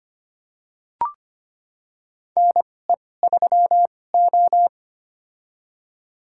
Morse code, heard as a keyed tone, about 25 wpm, spaced as for 18 wpm, 700 Hz. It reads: NE3O
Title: USA Call Signs - 4 Characters - 2x Character Spacing - 25wpm